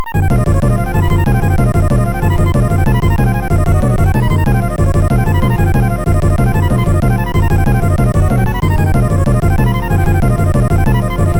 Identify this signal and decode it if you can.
There is no code.